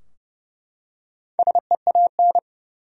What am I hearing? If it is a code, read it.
Morse code, heard as a keyed tone, about 32 wpm, 700 Hz. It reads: SEAN
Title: USA Common Names - 30wpm